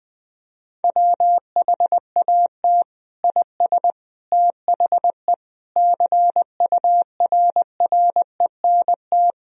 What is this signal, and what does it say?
Morse code, heard as a keyed tone, about 20 wpm, 700 Hz. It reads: WHAT IS THE CURRENT
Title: Sentences from Top 1000 Words - Review Part 4 - 20wpm